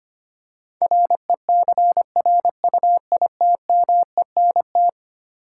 Morse code, heard as a keyed tone, about 25 wpm, 700 Hz. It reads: RECRUITMENT
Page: morse-code-ninja-practice-25wpm